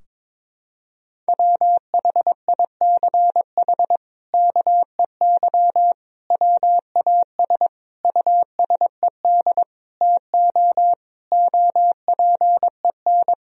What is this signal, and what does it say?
Morse code, heard as a keyed tone, about 22 wpm, 700 Hz. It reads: WHICH KEY WAS USED TO OPEN
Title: Sentences from Top 1000 Words - Review Part 4 - 22wpm